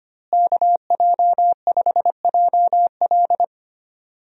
Morse code, heard as a keyed tone, about 25 wpm, 700 Hz. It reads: KJ5JL
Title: USA Call Signs - 5 Characters - Repeated 3x - 25wpm